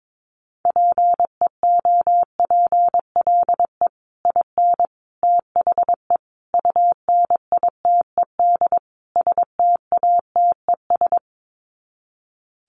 Morse code, heard as a keyed tone, about 22 wpm, 700 Hz. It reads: PEOPLE IN THE UNITED STATES
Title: Sets of 5 Words Encore 22wpm